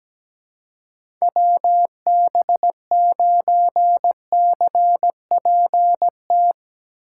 Morse code, heard as a keyed tone, about 18 wpm, 700 Hz. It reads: WB9CPT
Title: Speed Racing - USA 2x3 Call Signs 17wpm